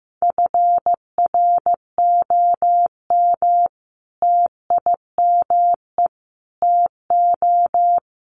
Morse code, heard as a keyed tone, about 15 wpm, 700 Hz. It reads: FROM TIME TO
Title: Sets of 4 Words 15wpm